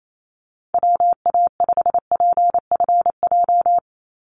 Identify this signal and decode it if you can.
Morse code, heard as a keyed tone, about 28 wpm, 700 Hz. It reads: WA5PFJ
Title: Call Signs Repeated 2x - 28wpm